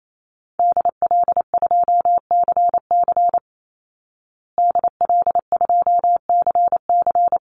Morse code, heard as a keyed tone, about 28 wpm, 700 Hz. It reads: DL2CC DL2CC
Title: Call Signs Repeated 2x - 28wpm